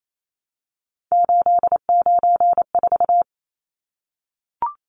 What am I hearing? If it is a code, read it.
Morse code, heard as a keyed tone, about 28 wpm, 700 Hz. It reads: 894
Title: Numbers - 3-Digit 28wpm